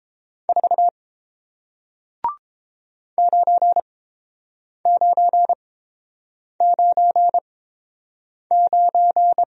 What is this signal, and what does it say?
Morse code, heard as a keyed tone, about 30 wpm, spaced as for 7 wpm, 700 Hz. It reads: 4 9999